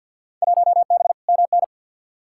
Morse code, heard as a keyed tone, about 51 wpm, 700 Hz. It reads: 1B MN
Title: Field Day Exchanges Encore - 50wpm